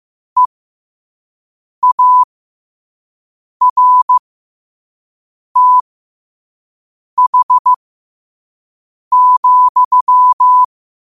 Morse code, heard as a keyed tone, about 15 wpm, spaced as for 5 wpm, 1000 Hz. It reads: EARTH,
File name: qotdfiles / 5wpm